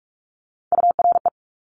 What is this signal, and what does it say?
Morse code, heard as a keyed tone, about 46 wpm, 700 Hz. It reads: URE
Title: Top Three Letter Combinations in Words 45wpm